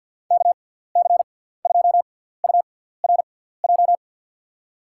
Morse code, heard as a keyed tone, about 52 wpm, spaced as for 17 wpm, 700 Hz. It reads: KC2URJ
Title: USA Call Signs - 6 Characters - 6x Character Spacing - 50wpm